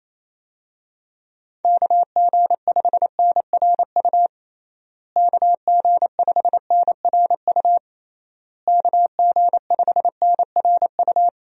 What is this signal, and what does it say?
Morse code, heard as a keyed tone, about 28 wpm, 700 Hz. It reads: KG5NRU KG5NRU KG5NRU
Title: USA Call Signs - 6 Characters - Repeated 3x - 28wpm